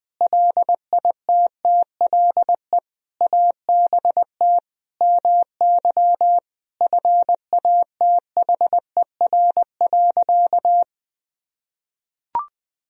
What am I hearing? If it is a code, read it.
Morse code, heard as a keyed tone, about 21 wpm, 700 Hz. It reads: LITTLE ABT MY FATHER.
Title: Sentences from Top 400 Words and Abbreviations 20wpm